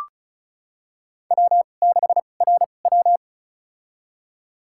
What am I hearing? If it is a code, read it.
Morse code, heard as a keyed tone, about 36 wpm, spaced as for 25 wpm, 700 Hz. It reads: W6RW
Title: USA Call Signs - 4 Characters - 2x Character Spacing - 35wpm